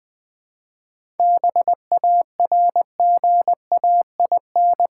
Morse code, heard as a keyed tone, about 20 wpm, 700 Hz. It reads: BARGAIN
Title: 7 Letter Words - First 3 Letters Given - 20wpm